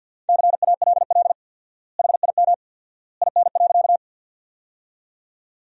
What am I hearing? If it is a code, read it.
Morse code, heard as a keyed tone, about 53 wpm, 700 Hz. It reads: CALL HIM IN.
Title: 4x Word-Spacing - Sentences from Top 100 Words - 50wpm